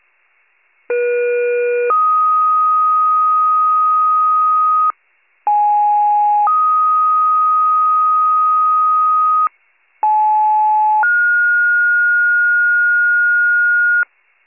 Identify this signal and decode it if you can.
Two-tone page, heard as a one-second tone, then a three-second tone, about 480 then 1250 Hz, about 830 then 1250 Hz, about 860 then 1430 Hz. It is heard from 0.9 to 14.0 s.